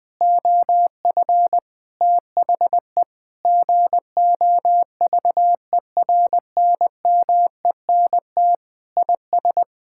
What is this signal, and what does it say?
Morse code, heard as a keyed tone, about 20 wpm, 700 Hz. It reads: OF THE GOVERNMENT IS
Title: Sentences with 7-Word Phrases - 20wpm